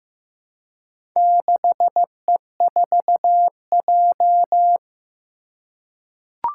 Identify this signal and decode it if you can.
Morse code, heard as a keyed tone, about 15 wpm, 700 Hz. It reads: 6E4J
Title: Four Challenging Letter-Number Combinations 15wpm